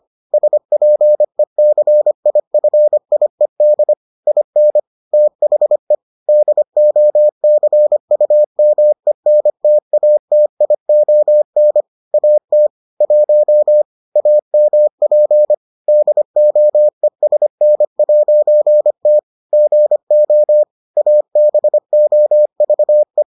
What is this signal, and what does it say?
Morse code, heard as a keyed tone, about 25 wpm, 600 Hz. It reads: SPECIFIED IN THE DOCUMENTATION AT 1 AMP DOESN'T GO ABOVE